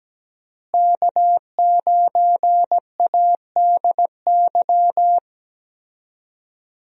Morse code, heard as a keyed tone, about 17 wpm, 700 Hz. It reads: K9ADY
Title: USA Call Signs - 5 Characters - Repeated 3x - 17wpm